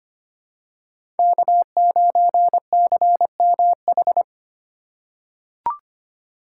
Morse code, heard as a keyed tone, about 25 wpm, 700 Hz. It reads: K9CMH
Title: USA Call Signs - 2x Character Spacing - 25wpm